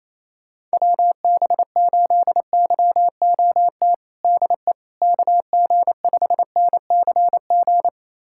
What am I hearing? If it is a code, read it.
Morse code, heard as a keyed tone, about 28 wpm, 700 Hz. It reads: WB8YOT DE KG5NCG
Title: USA Callsign DE Callsign 28wpm